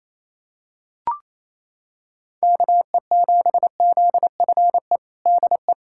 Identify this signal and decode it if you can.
Morse code, heard as a keyed tone, about 28 wpm, 700 Hz. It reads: KE7ZFE DE
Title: USA Callsign DE Callsign 28wpm